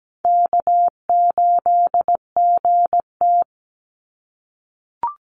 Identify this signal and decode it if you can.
Morse code, heard as a keyed tone, about 17 wpm, 700 Hz. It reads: K8GT
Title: Call Signs - 17wpm